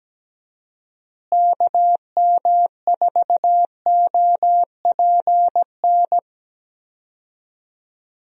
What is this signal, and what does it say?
Morse code, heard as a keyed tone, about 17 wpm, 700 Hz. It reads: KM4OPN
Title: USA Call Signs - 6 Characters - Repeated 2x - 17wpm